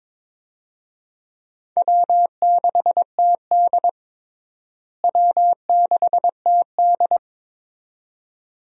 Morse code, heard as a keyed tone, about 22 wpm, 700 Hz. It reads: W6TD W6TD
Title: USA Call Signs - 4 Characters - Repeated 2x - 22wpm